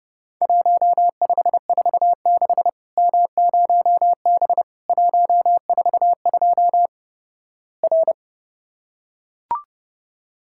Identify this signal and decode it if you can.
Morse code, heard as a keyed tone, about 32 wpm, 700 Hz. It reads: 1546 M0B 142 R
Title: Worked All Europe DX Contest - QTC Exchange - 30wpm